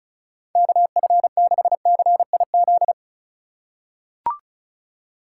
Morse code, heard as a keyed tone, about 36 wpm, 700 Hz. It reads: KF6CIZ
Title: USA Call Signs - 6 Characters - 3x Character Spacing - 35wpm